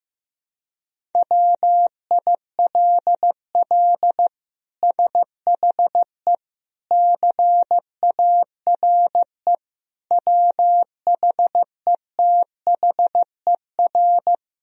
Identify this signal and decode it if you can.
Morse code, heard as a keyed tone, about 15 wpm, 700 Hz. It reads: WILL SHE CARE WHETHER
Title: Sentences from Top 1000 Words - Review Part 3 - 15wpm